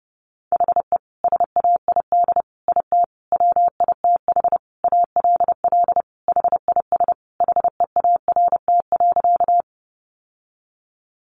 Morse code, heard as a keyed tone, about 30 wpm, 700 Hz. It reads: HE SAID IT WITH ALL HIS HEART.
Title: Sentences from Top 1200 Words 30wpm